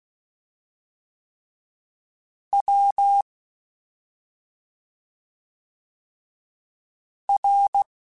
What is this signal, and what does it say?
Morse code, heard as a keyed tone, about 16 wpm, spaced as for 2 wpm, 800 Hz. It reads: WR
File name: Special INGs / Special INGs-mp3s / Fast ing @10